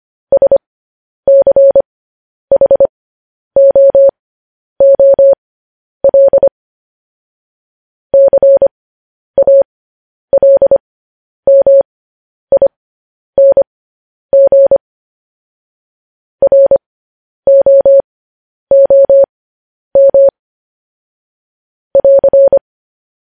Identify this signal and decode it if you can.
Morse code, heard as a keyed tone, about 25 wpm, spaced as for 10 wpm, 550 Hz. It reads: SCHOOL CALMING ROOM <AR>
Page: news headlines in morse code at 10 wpm